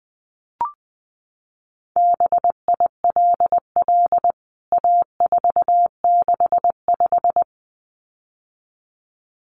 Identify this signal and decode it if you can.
Morse code, heard as a keyed tone, about 20 wpm, 700 Hz. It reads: BILL A465
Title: CWT Exchange 20wpm